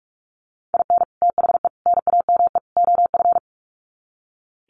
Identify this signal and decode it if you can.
Morse code, heard as a keyed tone, about 44 wpm, 700 Hz. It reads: IN THE NAME OF
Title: Sets of 4 Words 45wpm